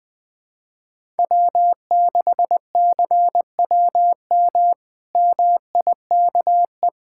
Morse code, heard as a keyed tone, about 20 wpm, 700 Hz. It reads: W6CWM MIKE
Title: K1USN SST QSO Practice - 20wpm